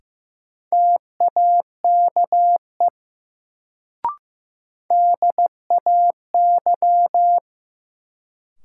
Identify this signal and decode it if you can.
Morse code, heard as a keyed tone, about 15 wpm, 700 Hz. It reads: TAKE DAY
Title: Sentences from Top 200 Words - New Word Drill - 15wpm